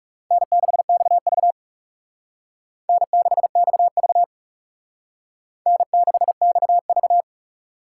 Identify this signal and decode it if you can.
Morse code, heard as a keyed tone, about 37 wpm, 700 Hz. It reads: N6XV N6XV N6XV
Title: Speed Racing - USA 1x2 Call Signs 30wpm